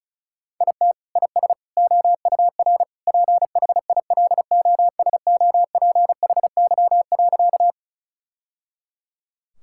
Morse code, heard as a keyed tone, about 34 wpm, 700 Hz. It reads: IT IS OUR PHILOSOPHY.
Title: Sentences from Top 1000 Words - Review Part 6 - 35wpm